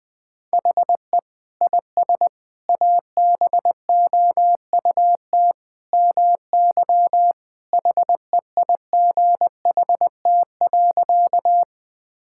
Morse code, heard as a keyed tone, about 21 wpm, 700 Hz. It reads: HE IS ABOUT MY HEIGHT.
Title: Sentences from Top 1600 Words 20wpm